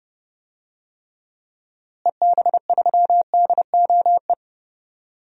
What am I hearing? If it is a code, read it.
Morse code, heard as a keyed tone, about 30 wpm, 700 Hz. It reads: EB3DOE
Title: DXCC-100 Call Signs - 6 Characters - Repeated 3x - 30wpm